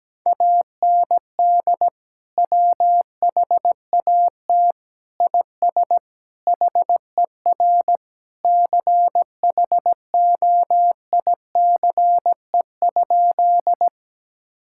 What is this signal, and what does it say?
Morse code, heard as a keyed tone, about 17 wpm, 700 Hz. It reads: AND WHAT IS HER CHOICE?
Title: Sentences from Top 1000 Words - Review Part 6 - 17wpm